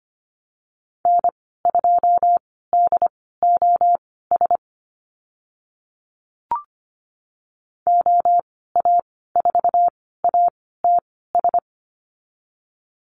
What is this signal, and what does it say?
Morse code, heard as a keyed tone, about 25 wpm, spaced as for 16 wpm, 700 Hz. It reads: N2DOS OA4ATS
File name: Lesson 15g - Number 5 - Call Signs 25wpm